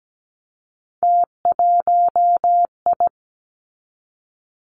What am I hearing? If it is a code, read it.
Morse code, heard as a keyed tone, about 17 wpm, 700 Hz. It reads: T1I
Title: Three Challenging Letter-Number Combinations 17wpm